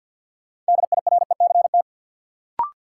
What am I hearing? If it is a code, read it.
Morse code, heard as a keyed tone, about 53 wpm, 700 Hz. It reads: DIRECT
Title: Sentences from Top 400 Words - New Word Drill - 50wpm